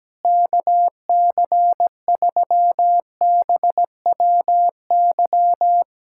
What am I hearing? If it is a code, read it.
Morse code, heard as a keyed tone, about 17 wpm, 700 Hz. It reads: KC3BWY
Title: USA Callsign DE Callsign - 6 Characters - 17wpm